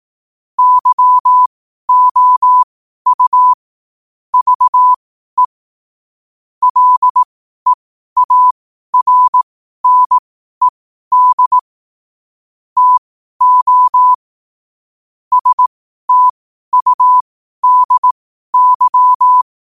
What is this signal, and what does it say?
Morse code, heard as a keyed tone, about 18 wpm, spaced as for 12 wpm, 1000 Hz. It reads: YOU VE LEARNED TO STUDY
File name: qotdfiles / 12wpm